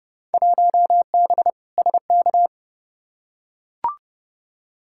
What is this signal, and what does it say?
Morse code, heard as a keyed tone, about 32 wpm, 700 Hz. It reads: 1B SK
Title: Field Day Exchanges Encore - 30wpm